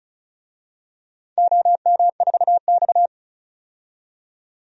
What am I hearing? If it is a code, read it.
Morse code, heard as a keyed tone, about 36 wpm, 700 Hz. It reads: OM4X
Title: DXCC-100 Call Signs - 4 Characters - 35wpm